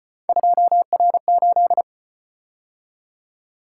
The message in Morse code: 2R8

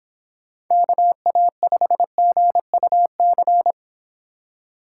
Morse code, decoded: KA5GUC

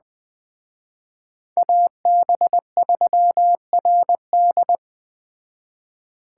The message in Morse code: AB3RD

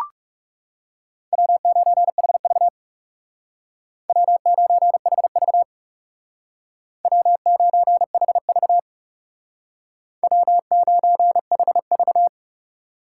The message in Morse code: W9HV W9HV W9HV W9HV

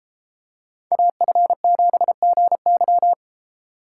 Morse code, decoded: AF7GY

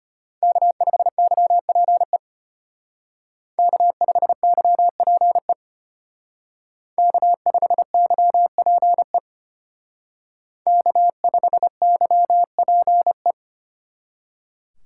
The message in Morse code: K5YPE K5YPE K5YPE K5YPE

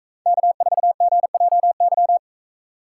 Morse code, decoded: K4GJY

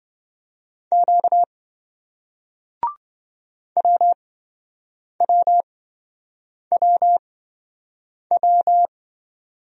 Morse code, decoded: Q WWWW

